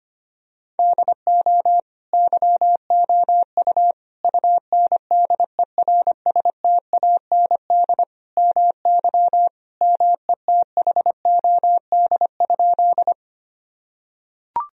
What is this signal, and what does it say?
Morse code, heard as a keyed tone, about 25 wpm, 700 Hz. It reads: DO YOU UNDERSTAND MY METHOD?
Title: Sentences from Top 1000 Words - Review Part 3 - 25wpm